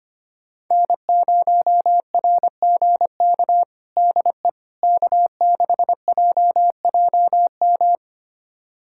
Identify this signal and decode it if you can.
Morse code, heard as a keyed tone, about 25 wpm, 700 Hz. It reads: N0RGK DE K6JJM
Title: USA Callsign DE Callsign - 5 Characters - 25wpm